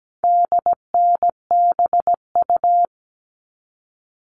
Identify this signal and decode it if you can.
Morse code, heard as a keyed tone, about 17 wpm, 700 Hz. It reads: DNBU
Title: All Cut Numbers - 4-Digit 17wpm